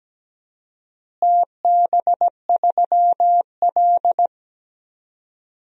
Morse code, heard as a keyed tone, about 17 wpm, 700 Hz. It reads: TB3L
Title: DXCC-100 Call Signs - 4 Characters - Repeated 3x - 17wpm